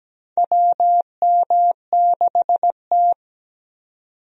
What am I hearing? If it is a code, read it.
Morse code, heard as a keyed tone, about 17 wpm, 700 Hz. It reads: WM6T
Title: USA Call Signs - 4 Characters - 17wpm